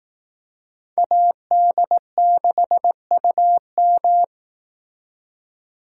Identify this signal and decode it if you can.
Morse code, heard as a keyed tone, about 18 wpm, 700 Hz. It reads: AD6UM
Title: Speed Racing - USA 2x2 Call Signs 15wpm